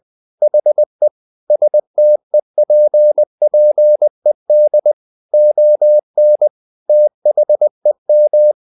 Morse code, heard as a keyed tone, about 20 wpm, 600 Hz. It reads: HE STEPPED ON THEM
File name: the old man and the sea / the old man and the sea part 1 20wpm